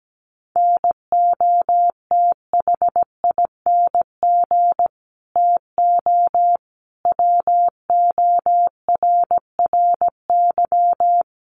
Morse code, heard as a keyed tone, about 17 wpm, 700 Hz. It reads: NOTHING TO WORRY
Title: Sentences from Top 1100 Words 17wpm